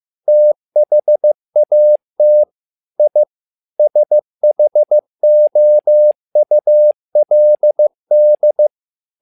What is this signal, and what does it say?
Morse code, heard as a keyed tone, about 15 wpm, 600 Hz. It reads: THAT I SHOULD
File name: learning center / tower trouble / tower trouble 15wpm